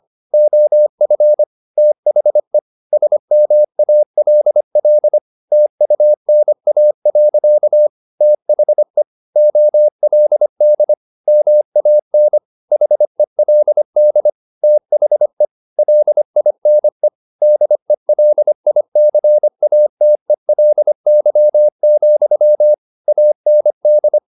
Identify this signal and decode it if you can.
Morse code, heard as a keyed tone, about 25 wpm, 600 Hz. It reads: OF THE SMALL TUNA. THE OLD MAN HELD THE LINE DELICATELY, AND
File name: the old man and the sea part 1 25wpm